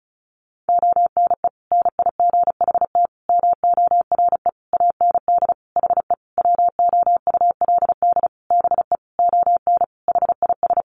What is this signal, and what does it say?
Morse code, heard as a keyed tone, about 34 wpm, 700 Hz. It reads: ONE NIGHT MORE AND HE WOULD BE ON HIS
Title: Sentences from Top 300 Words 35wpm